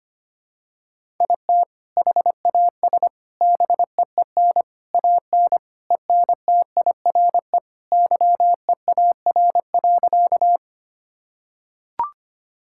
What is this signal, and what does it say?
Morse code, heard as a keyed tone, about 25 wpm, 700 Hz. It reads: IT HAS BEEN AN ENTIRE YEAR.
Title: Sentences from Top 1000 Words - Review Part 5 - 25wpm